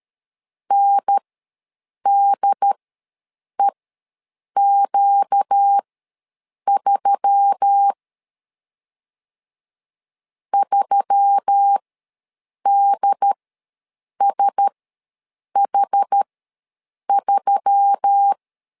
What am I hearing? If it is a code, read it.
Morse code, heard as a keyed tone, about 13 wpm, spaced as for 7 wpm, 800 Hz. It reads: NDEQ3 3DSH3
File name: LEZIONI / 4 Q1S3 ws 12wpm 2038